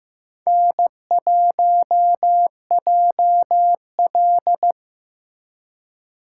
Morse code, heard as a keyed tone, about 18 wpm, 700 Hz. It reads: N1JL
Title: Speed Racing - USA 1x2 Call Signs 15wpm